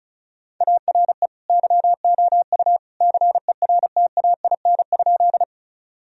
Morse code, heard as a keyed tone, about 36 wpm, 700 Hz. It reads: ARE YOU CERTAIN?
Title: Sentences from Top 1000 Words - Review Part 2 - 35wpm